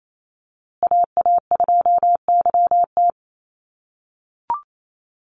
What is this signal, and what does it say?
Morse code, heard as a keyed tone, about 28 wpm, 700 Hz. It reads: AA2YT
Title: USA Call Signs Repeated 3x 28wpm